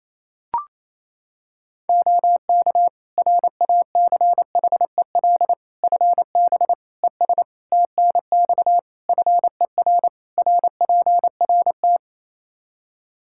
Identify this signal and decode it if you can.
Morse code, heard as a keyed tone, about 27 wpm, 700 Hz. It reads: OK RACHEL FB ES TNX FER RPRT
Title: K1USN SST QSO Practice 28wpm